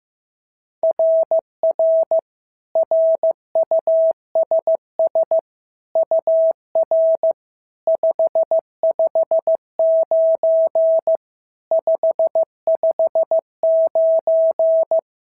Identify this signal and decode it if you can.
Morse code, heard as a keyed tone, about 14 wpm, 650 Hz. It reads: RR RUSS UR 559 559
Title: SOTA QSO Practice 15wpm